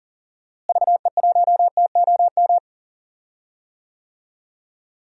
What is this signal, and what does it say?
Morse code, heard as a keyed tone, about 40 wpm, 700 Hz. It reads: VE1TOM
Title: DXCC-100 Call Signs - 6 Characters - Repeated 3x - 40wpm